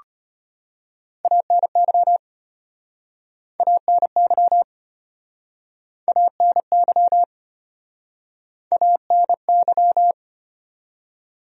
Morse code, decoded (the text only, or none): ANY ANY ANY ANY